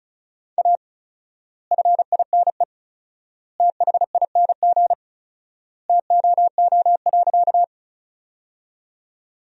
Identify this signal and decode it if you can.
Morse code, heard as a keyed tone, about 36 wpm, 700 Hz. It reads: A FINE THING TOO.
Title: 4x Word-Spacing - Sentences from Top 500 Words - 35wpm